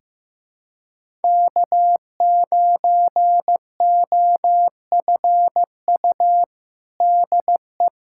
Morse code, heard as a keyed tone, about 15 wpm, 700 Hz. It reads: K9OFU DE
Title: USA Callsign DE Callsign - 5 Characters - 15wpm